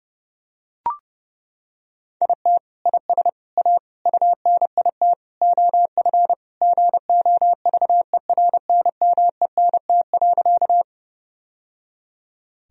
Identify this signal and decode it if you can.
Morse code, heard as a keyed tone, about 30 wpm, 700 Hz. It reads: IT IS A UNIT OF GOVERNMENT.
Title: Sentences from Top 1000 Words - Review Part 2 - 30wpm